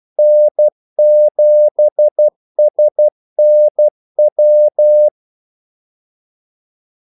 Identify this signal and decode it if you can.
Morse code, heard as a keyed tone, about 12 wpm, 600 Hz. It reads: N7SNW
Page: 2 ekstra ordmellemrum mellem kaldesignalerne